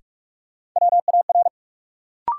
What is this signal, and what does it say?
Morse code, heard as a keyed tone, about 48 wpm, 700 Hz. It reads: WAR
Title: Sets of 5 Words - New Word Drill - 45wpm